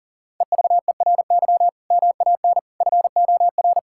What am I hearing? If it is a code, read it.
Morse code, heard as a keyed tone, about 40 wpm, 700 Hz. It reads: EVERY MAN FOR
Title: Sentences from Top 1600 Words 40wpm